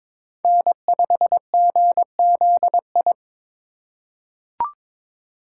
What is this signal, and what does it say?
Morse code, heard as a keyed tone, about 21 wpm, 700 Hz. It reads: N5GZI
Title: USA Call Signs - 5 Characters - 3x Character Spacing - 22wpm